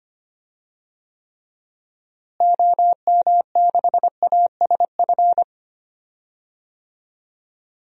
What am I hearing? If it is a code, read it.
Morse code, heard as a keyed tone, about 25 wpm, 700 Hz. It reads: OM6ASF
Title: DXCC-100 Call Signs - 6 Characters - Repeated 3x - 25wpm